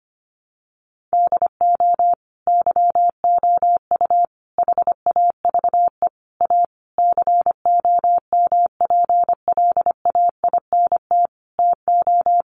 Morse code, heard as a keyed tone, about 25 wpm, 700 Hz. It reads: DO YOU HAVE A COMPLAINT TO
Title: Sentences from Top 1400 Words 25wpm